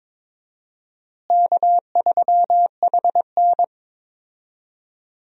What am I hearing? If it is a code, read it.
Morse code, heard as a keyed tone, about 21 wpm, 700 Hz. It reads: K3HN
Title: USA Call Signs - 4 Characters - Repeated 3x - 22wpm